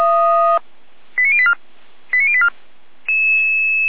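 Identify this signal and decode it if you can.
Five-tone selective call (ZVEI), heard as quick runs of five tones, tones of about 70 ms. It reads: 89084 89084
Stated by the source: ZVEI five-tone selective call received on an SDR.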